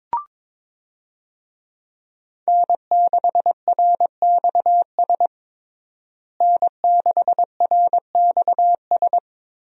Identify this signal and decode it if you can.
Morse code, heard as a keyed tone, about 22 wpm, 700 Hz. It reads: N6RXS N6RXS
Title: USA Call Signs - 5 Characters - Repeated 3x - 22wpm